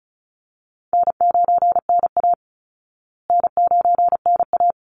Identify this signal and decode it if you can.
Morse code, heard as a keyed tone, about 34 wpm, 700 Hz. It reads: N9NA N9NA
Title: Easy Call Signs Repeated 2x - 35wpm